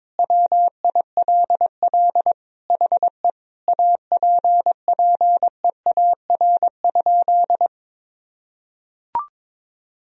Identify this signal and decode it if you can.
Morse code, heard as a keyed tone, about 21 wpm, 700 Hz. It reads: WILL HE APPEAR?
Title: Sentences from Top 1000 Words - Review Part 2 - 22wpm